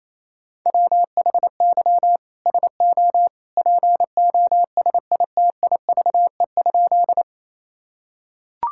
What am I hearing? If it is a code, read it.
Morse code, heard as a keyed tone, about 28 wpm, 700 Hz. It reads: WHY SO POSITIVE?
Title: Sentences from Top 1000 Words - Review Part 5 - 28wpm